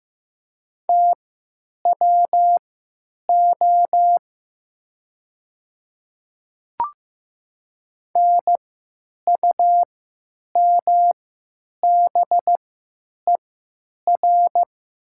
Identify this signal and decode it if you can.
Morse code, heard as a keyed tone, about 15 wpm, spaced as for 9 wpm, 690 Hz. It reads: TWO NUMBER
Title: 3x Character Spacing - Top 100 Words - 15wpm